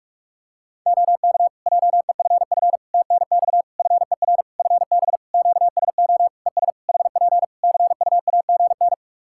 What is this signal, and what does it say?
Morse code, heard as a keyed tone, about 46 wpm, 700 Hz. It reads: OK JEFF TNX FER FB QSO ES HP CUAGN
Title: K1USN SST QSO Practice 45wpm